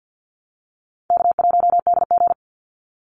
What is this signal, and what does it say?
Morse code, heard as a keyed tone, about 50 wpm, 700 Hz. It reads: K1DG
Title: Easy Call Signs Repeated 3x - 50wpm